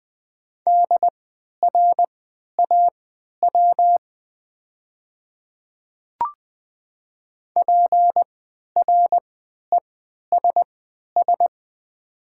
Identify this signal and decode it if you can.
Morse code, heard as a keyed tone, about 20 wpm, spaced as for 11 wpm, 700 Hz. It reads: DRAW PRESS
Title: 3x Character Spacing - Top 300 Words - 20wpm